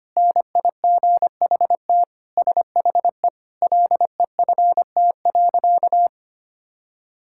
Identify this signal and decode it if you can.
Morse code, heard as a keyed tone, about 25 wpm, 700 Hz. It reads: NIGHT SHE LEFT.
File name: practice / 25wpm / Sentences with 4-Word Phrases - 25wpm